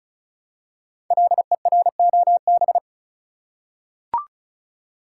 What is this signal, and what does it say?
Morse code, heard as a keyed tone, about 36 wpm, 700 Hz. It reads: LEROB
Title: Code Groups - 5 Letters - 35wpm